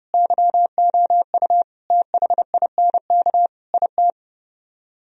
Morse code, heard as a keyed tone, about 33 wpm, 700 Hz. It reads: YOU THINK IT